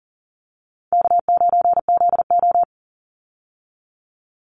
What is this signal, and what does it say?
Morse code, heard as a keyed tone, about 40 wpm, 700 Hz. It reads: K9ZO
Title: Easy Call Signs - 40wpm